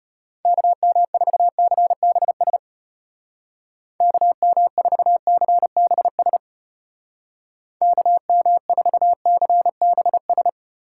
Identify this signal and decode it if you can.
Morse code, heard as a keyed tone, about 30 wpm, 700 Hz. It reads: KM4CBS KM4CBS KM4CBS
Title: Speed Racing - USA 2x3 Call Signs 25wpm